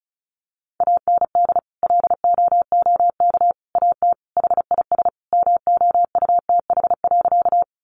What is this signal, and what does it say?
Morse code, heard as a keyed tone, about 34 wpm, 700 Hz. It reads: AND LOOK AT HIS MOUTH.
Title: Sentences from Top 1500 Words 35wpm